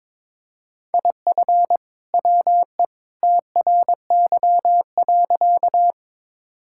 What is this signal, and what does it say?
Morse code, heard as a keyed tone, about 22 wpm, 700 Hz. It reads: IF WE TRY.
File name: practice / 22wpm / Sentences with 2-Word Phrases - 22wpm